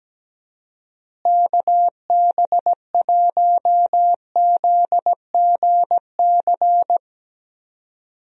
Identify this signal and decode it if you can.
Morse code, heard as a keyed tone, about 17 wpm, 700 Hz. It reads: KB1ZGC
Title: USA Call Signs - 6 Characters - Repeated 2x - 17wpm